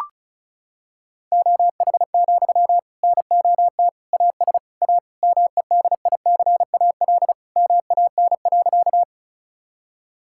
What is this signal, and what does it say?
Morse code, heard as a keyed tone, about 36 wpm, 700 Hz. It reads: OH, NOT AS A MEDICAL MAN.
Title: Sentences from Top 1000 Words - Review Part 5 - 35wpm